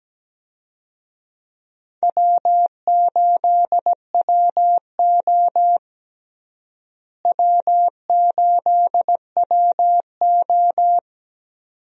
Morse code, heard as a keyed tone, about 17 wpm, 700 Hz. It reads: W8WO W8WO
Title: USA Call Signs - 4 Characters - Repeated 2x - 17wpm